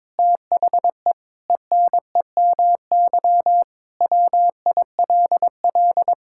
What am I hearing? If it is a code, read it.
Morse code, heard as a keyed tone, about 22 wpm, 700 Hz. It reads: THE ENEMY WILL
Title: Pangram Challenge v2 - 22wpm